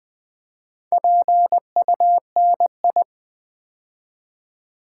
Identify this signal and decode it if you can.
Morse code, heard as a keyed tone, about 20 wpm, 700 Hz. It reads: PUNI
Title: Build A Word - 4 Letters At A Time - 10-Letter Words - 20wpm